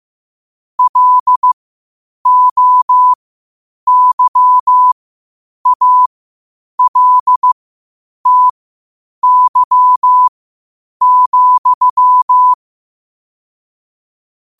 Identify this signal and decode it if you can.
Morse code, heard as a keyed tone, about 15 wpm, spaced as for 8 wpm, 1000 Hz. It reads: LOYALTY,
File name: qotdfiles / 8wpm